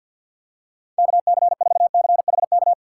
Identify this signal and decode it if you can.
Morse code, heard as a keyed tone, about 53 wpm, 700 Hz. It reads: KC4CHK
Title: USA Call Signs - 2x Character Spacing - 50wpm